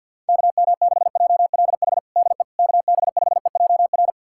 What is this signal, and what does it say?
Morse code, heard as a keyed tone, about 51 wpm, 700 Hz. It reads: KM6JLH DE KB5EJR
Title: USA Callsign DE Callsign - 6 Characters - 50wpm